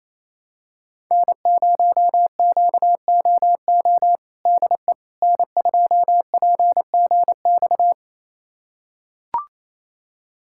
Morse code, heard as a keyed tone, about 28 wpm, 700 Hz. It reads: N0QOO DE N2PGX
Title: USA Callsign DE Callsign - 5 Characters - 28wpm